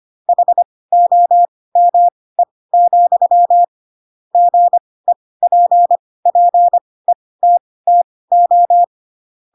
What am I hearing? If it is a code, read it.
Morse code, heard as a keyed tone, about 25 wpm, spaced as for 18 wpm, 700 Hz. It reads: HOME, GEPPETTO
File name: Pinocchio ch3 12 25 ik1hge